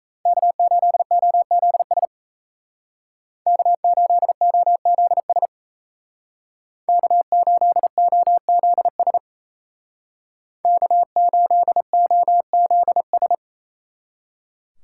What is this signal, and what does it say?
Morse code, heard as a keyed tone, about 37 wpm, 700 Hz. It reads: K8OZS K8OZS K8OZS K8OZS